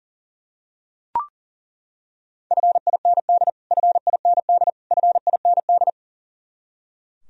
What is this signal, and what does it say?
Morse code, heard as a keyed tone, about 40 wpm, 700 Hz. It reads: FIND FIND FIND
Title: Top 100 Words Repeated 40wpm